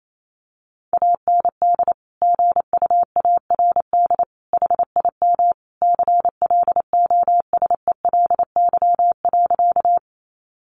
Morse code, heard as a keyed tone, about 28 wpm, 700 Hz. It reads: AND GUARD HIM CLOSELY.
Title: Sentences from Top 1400 Words 28wpm